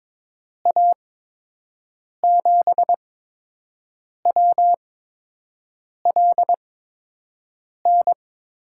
Morse code, heard as a keyed tone, about 22 wpm, spaced as for 6 wpm, 700 Hz. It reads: A7WLN
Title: USA Call Signs - 6 Characters - 8x Character Spacing - 22wpm